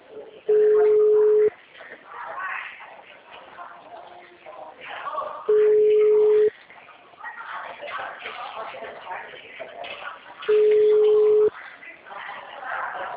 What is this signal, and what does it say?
Ringback tone (Europe, 425 Hz), with speech.
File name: out- -2001- - - .2113